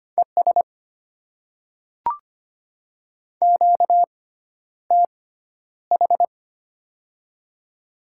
Morse code, heard as a keyed tone, about 25 wpm, spaced as for 11 wpm, 690 Hz. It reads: ES QTH
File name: Top 20 QSO Elements - 6x Character Spacing - 25wpm